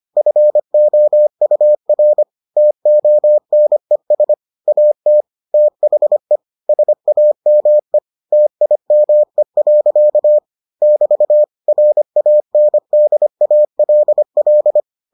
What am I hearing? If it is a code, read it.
Morse code, heard as a keyed tone, about 25 wpm, 600 Hz. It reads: FOUR TONES AT THE SAME TIME. <BT> RANDALL